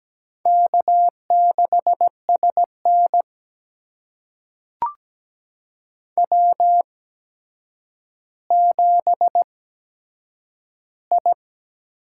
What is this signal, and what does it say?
Morse code, heard as a keyed tone, about 17 wpm, spaced as for 8 wpm, 700 Hz. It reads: K6SN W7I
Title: USA Call Signs - 4 Characters - 8x Character Spacing - 17wpm